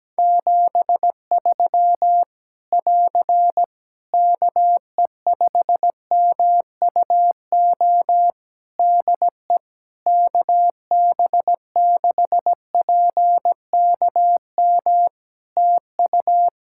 Morse code, heard as a keyed tone, about 17 wpm, 700 Hz. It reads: 73 <AR> KE5MUO DE KB6PKM TU
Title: Practice QSO - Exchange 4 - Ending - 17wpm